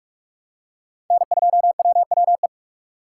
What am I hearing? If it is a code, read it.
Morse code, heard as a keyed tone, about 45 wpm, 700 Hz. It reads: N2WWE
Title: USA Call Signs - 5 Characters - 3x Character Spacing - 45wpm